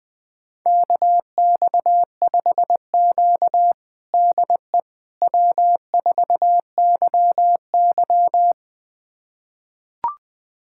Morse code, heard as a keyed tone, about 20 wpm, 700 Hz. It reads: KX5Q DE W4YY
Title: USA Callsign DE Callsign - 4 Characters - 20wpm